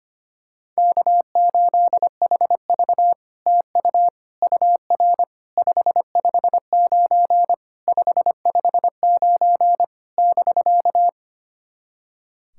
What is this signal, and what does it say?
Morse code, heard as a keyed tone, about 25 wpm, 700 Hz. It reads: K8HV TU UR 559 559 <BK>
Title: SOTA QSO Practice 25wpm